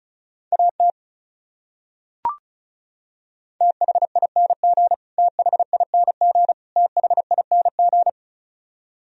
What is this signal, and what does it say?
Morse code, heard as a keyed tone, about 36 wpm, 700 Hz. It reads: AT THING THING THING
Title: Top 100 Words Repeated 35wpm